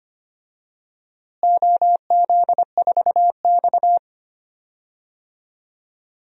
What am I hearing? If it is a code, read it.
Morse code, heard as a keyed tone, about 25 wpm, 700 Hz. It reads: OZ4X
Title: DXCC-100 Call Signs - 4 Characters - Repeated 3x - 25wpm